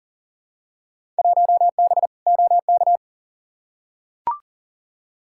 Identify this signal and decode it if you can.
Morse code, heard as a keyed tone, about 40 wpm, 700 Hz. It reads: 1B OK
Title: Field Day Exchanges Encore - 40wpm